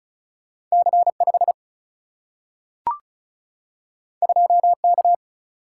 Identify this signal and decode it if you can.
Morse code, heard as a keyed tone, about 36 wpm, 700 Hz. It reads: C5 2K
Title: Single-Letter Single-Number 35wpm